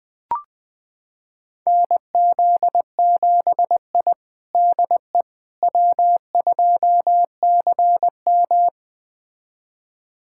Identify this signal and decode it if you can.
Morse code, heard as a keyed tone, about 20 wpm, 700 Hz. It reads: NZ7I DE W2CM
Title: USA Callsign DE Callsign - 4 Characters - 20wpm